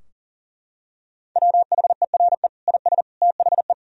Morse code, heard as a keyed tone, about 40 wpm, 700 Hz. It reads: WHERE IS THE
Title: Sentences from Top 1000 Words - Review Part 4 - 40wpm